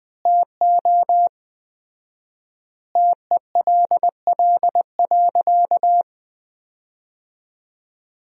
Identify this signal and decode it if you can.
Morse code, heard as a keyed tone, about 20 wpm, 700 Hz. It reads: TO TELL.
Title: 4x Word-Spacing - Sentences from Top 200 Words - 20wpm